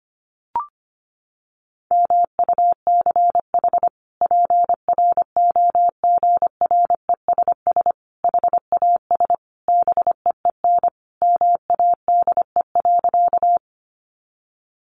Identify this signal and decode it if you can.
Morse code, heard as a keyed tone, about 25 wpm, 700 Hz. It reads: MUCH PROGRESS HAS BEEN MADE.
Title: Sentences from Top 1400 Words 25wpm